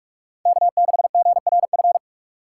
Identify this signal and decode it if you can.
Morse code, heard as a keyed tone, about 44 wpm, 690 Hz. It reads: K6GRF